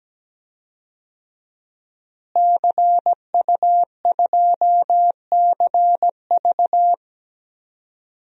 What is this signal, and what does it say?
Morse code, heard as a keyed tone, about 17 wpm, 700 Hz. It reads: CU2CV
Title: DXCC-100 Call Signs - 5 Characters - Repeated 3x - 17wpm